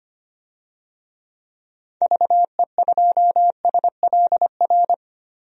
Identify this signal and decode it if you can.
Morse code, heard as a keyed tone, about 25 wpm, 700 Hz. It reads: VE2SLR
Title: DXCC-100 Call Signs - 6 Characters - 25wpm